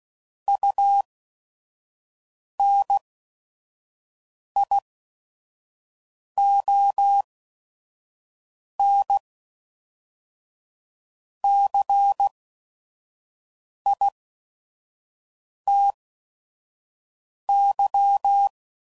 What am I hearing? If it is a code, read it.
Morse code, heard as a keyed tone, about 16 wpm, spaced as for 5 wpm, 800 Hz. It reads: UNIONCITY
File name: CW Code Groups @5WPM / fromSM's / Cities